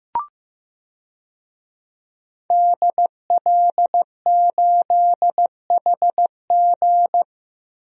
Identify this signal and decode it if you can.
Morse code, heard as a keyed tone, about 15 wpm, 700 Hz. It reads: DL8HG
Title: DXCC-100 Call Signs - 5 Characters - 15wpm